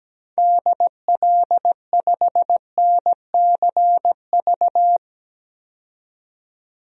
Morse code, heard as a keyed tone, about 17 wpm, 700 Hz. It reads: DL5NCV
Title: DXCC-100 Call Signs - 6 Characters - 17wpm